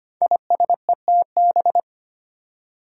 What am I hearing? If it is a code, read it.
Morse code, heard as a keyed tone, about 25 wpm, 700 Hz. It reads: ISETB
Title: Five Challenging Letter-Number Combinations 25wpm